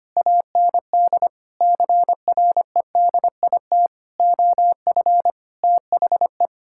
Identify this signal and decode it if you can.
Morse code, heard as a keyed tone, about 25 wpm, 700 Hz. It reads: AND CREDIT OF THE
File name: Sets of 9 Words - 25wpm